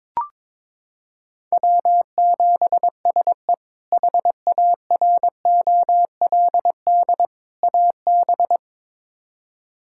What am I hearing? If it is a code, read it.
Morse code, heard as a keyed tone, about 22 wpm, 700 Hz. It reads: W7SE HAROLD AB
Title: K1USN SST QSO Practice - 22wpm